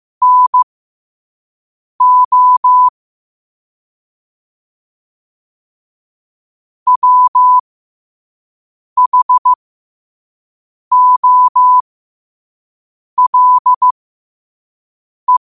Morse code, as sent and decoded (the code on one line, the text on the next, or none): -. --- / .-- .... --- .-.. .
NO WHOLE